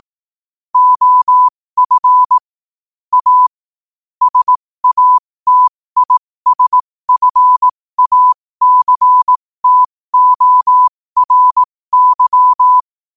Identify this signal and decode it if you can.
Morse code, heard as a keyed tone, about 18 wpm, spaced as for 15 wpm, 1000 Hz. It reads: OF A SATISFACTORY
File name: qotdfiles / 15wpm